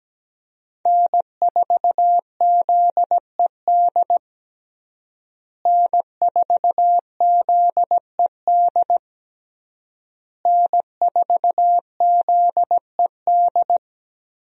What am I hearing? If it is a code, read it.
Morse code, heard as a keyed tone, about 17 wpm, 700 Hz. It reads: N4ZED N4ZED N4ZED
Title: USA Call Signs - 5 Characters - Repeated 3x - 17wpm